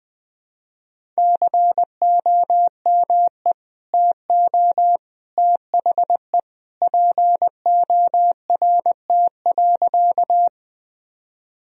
Morse code, heard as a keyed tone, about 21 wpm, 700 Hz. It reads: COME TO THE PORT.
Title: Sentences from Top 1000 Words - Review Part 1 - 20wpm